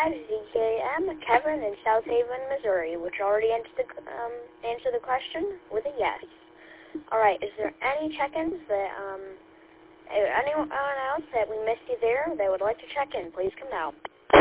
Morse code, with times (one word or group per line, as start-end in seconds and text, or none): none